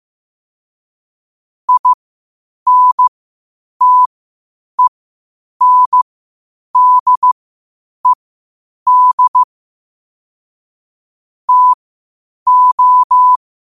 Morse code, heard as a keyed tone, about 15 wpm, spaced as for 8 wpm, 1000 Hz. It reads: INTENDED TO
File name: qotdfiles / 8wpm